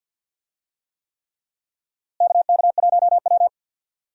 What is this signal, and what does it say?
Morse code, heard as a keyed tone, about 50 wpm, 700 Hz. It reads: KK1W
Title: USA Call Signs - 4 Characters - Repeated 3x - 50wpm